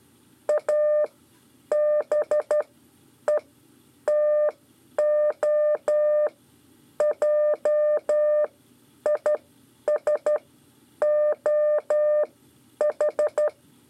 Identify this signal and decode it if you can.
Morse code, heard as a keyed tone, about 13 wpm, spaced as for 9 wpm, 600 Hz. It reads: ABETOJISOH
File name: 5 LEZIONE RIPASSO 1